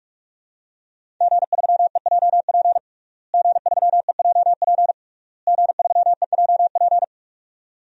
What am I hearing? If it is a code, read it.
Morse code, heard as a keyed tone, about 46 wpm, 700 Hz. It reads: G3EJP G3EJP G3EJP
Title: DXCC-100 Call Signs - 5 Characters - Repeated 3x - 45wpm